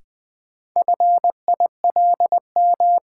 Morse code, heard as a keyed tone, about 20 wpm, 700 Hz. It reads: FILM
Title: Sentences from Top 1000 Words - Review Part 3 - New Word Drill - 20wpm